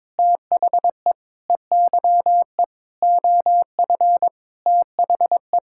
Morse code, heard as a keyed tone, about 22 wpm, 700 Hz. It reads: THE EYE OF THE
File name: practice / 22wpm / Sentences with 8-Word Phrases - 22wpm